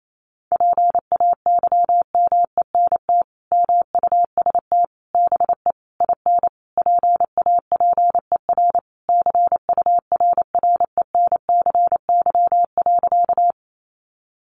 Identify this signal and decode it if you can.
Morse code, heard as a keyed tone, about 28 wpm, 700 Hz. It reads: PAYMENT MUST BE IN PAPER CURRENCY.
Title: Sentences from Top 1500 Words 28wpm